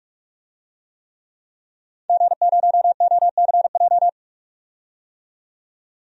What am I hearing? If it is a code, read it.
Morse code, heard as a keyed tone, about 46 wpm, 700 Hz. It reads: G0OCJ
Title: DXCC-100 Call Signs - 5 Characters - 45wpm